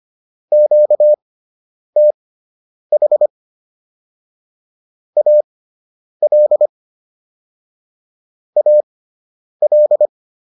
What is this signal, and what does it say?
Morse code, heard as a keyed tone, about 25 wpm, spaced as for 9 wpm, 600 Hz. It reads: QTH AL AL